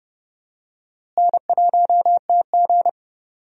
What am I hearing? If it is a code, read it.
Morse code, heard as a keyed tone, about 30 wpm, 700 Hz. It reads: N1TG
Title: USA Call Signs - 4 Characters - Repeated 3x - 30wpm